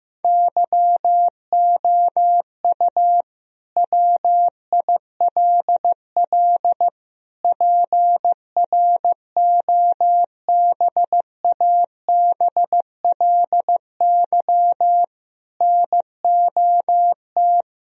Morse code, heard as a keyed tone, about 16 wpm, 700 Hz. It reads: YOU WILL PROBABLY NOT
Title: Sentences from Top 1000 Words - Review Part 4 - 15wpm